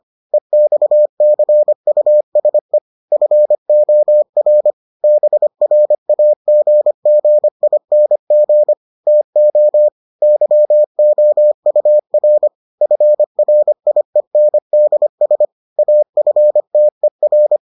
Morse code, heard as a keyed tone, about 25 wpm, 600 Hz. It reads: EXCUSE FOR BRAGGING TO YOUR FRIENDS AFTER